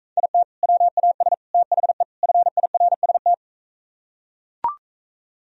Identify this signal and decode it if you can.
Morse code, heard as a keyed tone, about 42 wpm, 700 Hz. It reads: IT WAS THE FIRST